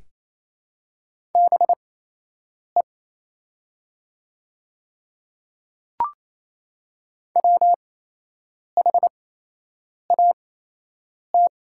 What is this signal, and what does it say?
Morse code, heard as a keyed tone, about 27 wpm, spaced as for 8 wpm, 700 Hz. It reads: BE WHAT
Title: 8x Character Spacing - Top 100 Words - 28wpm